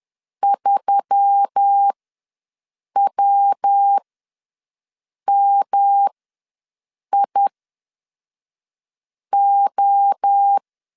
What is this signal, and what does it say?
Morse code, heard as a keyed tone, about 10 wpm, spaced as for 5 wpm, 800 Hz. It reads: 3WMIO
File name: LEZIONI / 5 LBVC / 5 LBVC ws 10wpm 3733